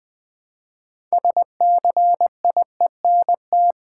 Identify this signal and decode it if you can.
Morse code, heard as a keyed tone, about 20 wpm, 700 Hz. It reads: SCIENT
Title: Build A Word - 3 Letters At A Time - 9-Letter Words - 20wpm